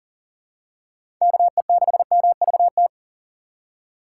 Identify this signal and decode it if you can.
Morse code, heard as a keyed tone, about 40 wpm, 700 Hz. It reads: KE6MVT
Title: USA Call Signs - 6 Characters - 6x Character Spacing - 40wpm